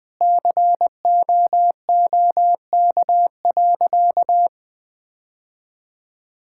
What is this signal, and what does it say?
Morse code, heard as a keyed tone, about 20 wpm, 700 Hz. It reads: COOK.
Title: Sentences with 2-Word Phrases - 20wpm